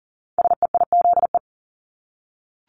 Morse code, heard as a keyed tone, about 40 wpm, 700 Hz. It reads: SEIZE
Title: Multi-letter Phonemes Repeated 3x 40wpm